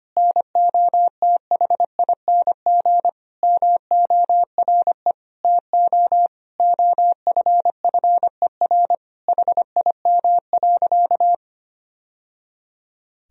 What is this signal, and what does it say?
Morse code, heard as a keyed tone, about 25 wpm, 700 Hz. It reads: NOTHING MORE TO OFFER HIM.
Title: Sentences from Top 1000 Words - Review Part 4 - 25wpm